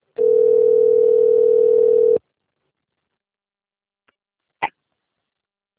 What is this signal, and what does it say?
Ringback tone (North America), 440 + 480 Hz.